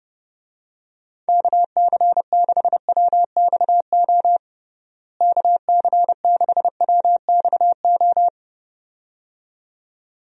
Morse code, heard as a keyed tone, about 32 wpm, 700 Hz. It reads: KC6WXO KC6WXO
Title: USA Call Signs - 6 Characters - Repeated 2x - 30wpm